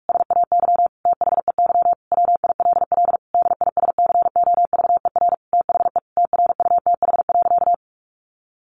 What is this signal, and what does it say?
Morse code, heard as a keyed tone, about 46 wpm, 700 Hz. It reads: SAY THEY WILL DISCOVER THE TRUTH.
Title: Sentences from Top 1100 Words 45wpm